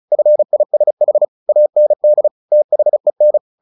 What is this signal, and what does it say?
Morse code, heard as a keyed tone, about 34 wpm, 600 Hz. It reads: FISH AND THEN
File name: the old man and the sea / the old man and the sea 1 35wpm